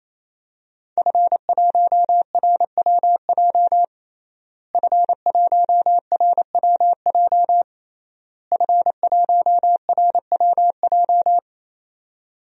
Morse code, heard as a keyed tone, about 28 wpm, 700 Hz. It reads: F1RWJ F1RWJ F1RWJ
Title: DXCC-100 Call Signs - 5 Characters - Repeated 3x - 28wpm